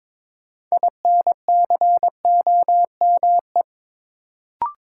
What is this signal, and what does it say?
Morse code, heard as a keyed tone, about 23 wpm, 700 Hz. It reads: INCOME
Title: Sentences from Top 1000 Words - Review Part 5 - New Word Drill - 22wpm